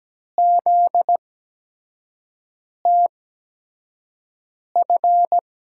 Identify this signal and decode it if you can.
Morse code, heard as a keyed tone, about 17 wpm, spaced as for 5 wpm, 700 Hz. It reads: ZTF
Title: USA Call Signs - 5 Characters - 8x Character Spacing - 17wpm